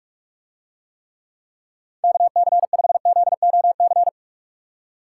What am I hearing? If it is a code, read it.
Morse code, heard as a keyed tone, about 45 wpm, 700 Hz. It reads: KC5ZOC